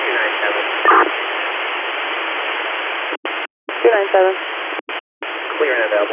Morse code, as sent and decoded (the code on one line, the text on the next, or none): none